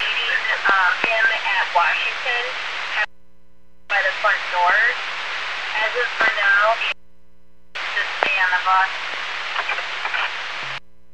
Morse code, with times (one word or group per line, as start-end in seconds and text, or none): none